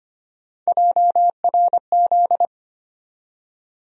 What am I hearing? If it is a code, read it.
Morse code, heard as a keyed tone, about 25 wpm, 700 Hz. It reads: JRZ
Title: USA Call Sign Postfixes - 3 Letters 25wpm